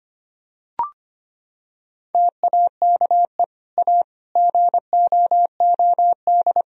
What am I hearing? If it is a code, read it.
Morse code, heard as a keyed tone, about 25 wpm, 700 Hz. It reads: TAKE A GOOD
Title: Sentences from Top 1000 Words - Review Part 3 - 25wpm